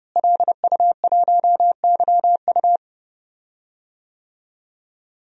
Morse code, heard as a keyed tone, about 32 wpm, 700 Hz. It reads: LU1YU
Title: DXCC-100 Call Signs - 5 Characters - Repeated 2x - 30wpm